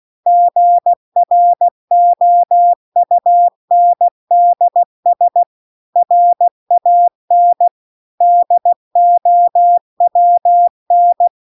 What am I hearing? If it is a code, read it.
Morse code, heard as a keyed tone, about 16 wpm, 700 Hz. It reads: GROUNDS RAN DOWN